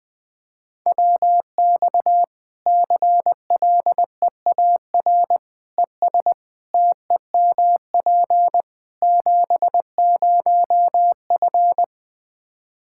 Morse code, heard as a keyed tone, about 20 wpm, 700 Hz. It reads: WX CLEAR ES TEMP 70F
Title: K1USN SST QSO Practice 20wpm